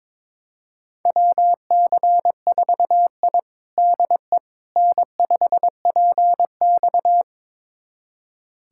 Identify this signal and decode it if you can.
Morse code, heard as a keyed tone, about 22 wpm, 700 Hz. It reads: WC4I DE N5PX
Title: USA Callsign DE Callsign - 4 Characters - 22wpm